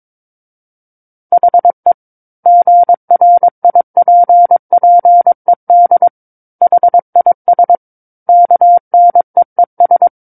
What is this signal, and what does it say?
Morse code, heard as a keyed tone, about 22 wpm, 700 Hz. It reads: HE GRIPPED HIS KNEES
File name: Book One / 13-Chapter 13 - 22 WPM